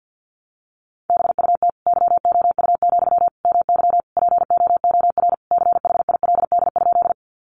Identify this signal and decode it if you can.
Morse code, heard as a keyed tone, about 50 wpm, 700 Hz. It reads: BUT YOU, MY POOR CHILD?
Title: Sentences from Top 1000 Words 50wpm